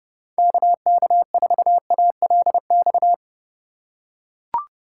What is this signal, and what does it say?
Morse code, heard as a keyed tone, about 30 wpm, 700 Hz. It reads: KK4ALX
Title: USA Call Signs - 6 Characters - 2x Character Spacing - 30wpm